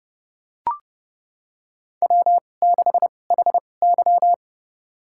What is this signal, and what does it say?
Morse code, heard as a keyed tone, about 30 wpm, spaced as for 22 wpm, 700 Hz. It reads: W6HY
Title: USA Call Signs - 4 Characters - 2x Character Spacing - 30wpm